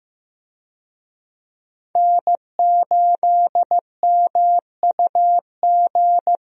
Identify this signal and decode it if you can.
Morse code, heard as a keyed tone, about 15 wpm, 700 Hz. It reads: N8MUG
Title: USA Call Signs - 5 Characters - Repeated 3x - 15wpm